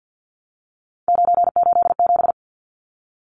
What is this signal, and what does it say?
Morse code, heard as a keyed tone, about 50 wpm, 700 Hz. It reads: 987
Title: Numbers - 3-Digit 50wpm